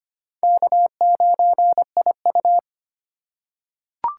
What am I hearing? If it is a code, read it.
Morse code, heard as a keyed tone, about 25 wpm, 700 Hz. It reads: K9IU
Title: USA Call Signs - 4 Characters - 2x Character Spacing - 25wpm